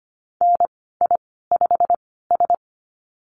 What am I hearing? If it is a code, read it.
Morse code, heard as a keyed tone, about 25 wpm, spaced as for 16 wpm, 700 Hz. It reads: NI5S
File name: Lesson 15g - Number 5 - Call Signs 25wpm